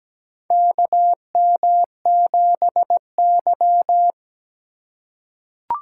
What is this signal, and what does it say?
Morse code, heard as a keyed tone, about 17 wpm, 700 Hz. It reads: KM7Y
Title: USA Call Signs - 4 Characters - 2x Character Spacing - 17wpm